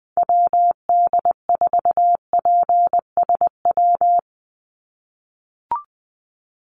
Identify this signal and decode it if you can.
Morse code, heard as a keyed tone, about 20 wpm, 700 Hz. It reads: WD4PSW
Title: USA Call Signs 20wpm